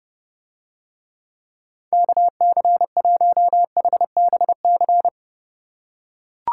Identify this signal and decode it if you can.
Morse code, heard as a keyed tone, about 30 wpm, 700 Hz. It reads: KC1HBC